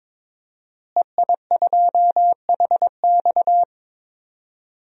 Morse code, heard as a keyed tone, about 22 wpm, 700 Hz. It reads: EI2HX
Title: DXCC-100 Call Signs - 5 Characters - Repeated 2x - 22wpm